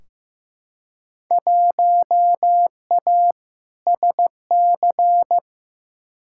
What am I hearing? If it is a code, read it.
Morse code, heard as a keyed tone, about 15 wpm, 700 Hz. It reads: 1A SC
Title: Field Day Exchanges - 15wpm